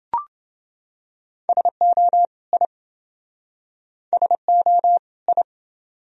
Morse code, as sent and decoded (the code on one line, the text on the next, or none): ... --- / .. / ... --- / ..
SO I SO I